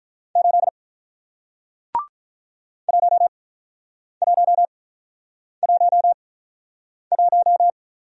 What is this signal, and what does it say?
Morse code, heard as a keyed tone, about 48 wpm, spaced as for 9 wpm, 700 Hz. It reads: 8 1111